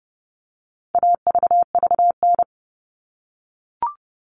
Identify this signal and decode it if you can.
Morse code, heard as a keyed tone, about 30 wpm, 700 Hz. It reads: AVVN
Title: All Cut Numbers - 4-Digit 30wpm